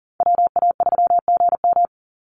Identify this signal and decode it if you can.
Morse code, heard as a keyed tone, about 40 wpm, 700 Hz. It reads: WA3GM
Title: Call Signs - 40wpm